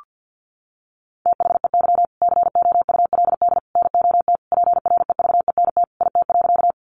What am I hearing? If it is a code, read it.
Morse code, heard as a keyed tone, about 52 wpm, 700 Hz. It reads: THEY COULD NOT PREVENT IT.
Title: Sentences from Top 1100 Words 50wpm